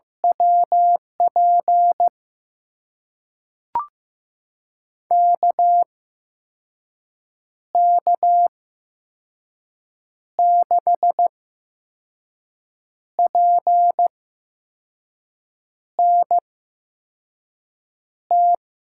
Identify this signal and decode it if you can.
Morse code, heard as a keyed tone, about 15 wpm, spaced as for 5 wpm, 700 Hz. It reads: WP KK6PNT
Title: USA Call Signs - 6 Characters - 8x Character Spacing - 15wpm